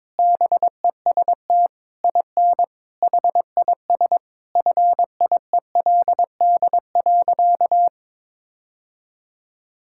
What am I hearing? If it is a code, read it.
Morse code, heard as a keyed tone, about 22 wpm, 700 Hz. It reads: BEST IN HIS FIELD.
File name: Sentences with 7-Word Phrases - 22wpm